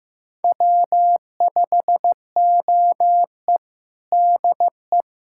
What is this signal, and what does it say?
Morse code, heard as a keyed tone, about 15 wpm, 700 Hz. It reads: W5OE DE
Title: USA Callsign DE Callsign - 4 Characters - 15wpm